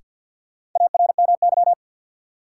Morse code, heard as a keyed tone, about 48 wpm, 700 Hz. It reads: ARMY
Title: Sentences from Top 800 Words - New Word Drill - 50wpm